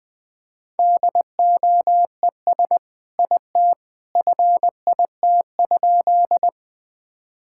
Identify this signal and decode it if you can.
Morse code, heard as a keyed tone, about 20 wpm, 700 Hz. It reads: DOES IT FIT?
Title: Sentences from Top 1000 Words - Review Part 5 - 20wpm